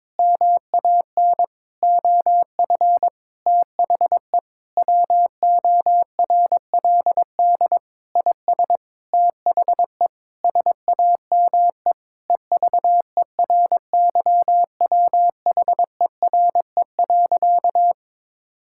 Morse code, heard as a keyed tone, about 22 wpm, 700 Hz. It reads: MAN OF THE WORLD IS THE SAME EVERYWHERE.
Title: Sentences from Top 1600 Words 22wpm